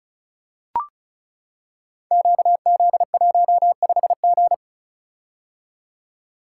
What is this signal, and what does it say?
Morse code, heard as a keyed tone, about 36 wpm, 700 Hz. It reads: QZ15G
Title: Code Groups - 5 Letter:Numbers - 35wpm